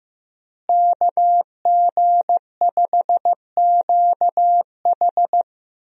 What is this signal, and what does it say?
Morse code, heard as a keyed tone, about 16 wpm, 700 Hz. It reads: KG5QH
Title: USA Call Signs - 5 Characters - Repeated 3x - 15wpm